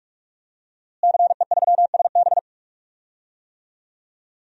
Morse code, heard as a keyed tone, about 45 wpm, 700 Hz. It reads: CE3SB
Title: DXCC-100 Call Signs - 5 Characters - Repeated 2x - 45wpm